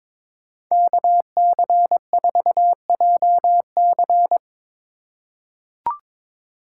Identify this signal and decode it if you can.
Morse code, heard as a keyed tone, about 22 wpm, 700 Hz. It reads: KC4JC
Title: USA Call Signs - 3x Character Spacing - 22wpm